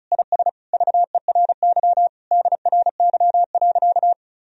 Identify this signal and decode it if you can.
Morse code, heard as a keyed tone, about 36 wpm, 700 Hz. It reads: IS VERY DRY.
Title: Sentences from Top 1000 Words - Review Part 4 - 35wpm